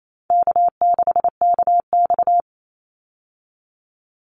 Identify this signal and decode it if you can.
Morse code, heard as a keyed tone, about 28 wpm, 700 Hz. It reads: K6KX
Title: Easy Call Signs Repeated 2x - 28wpm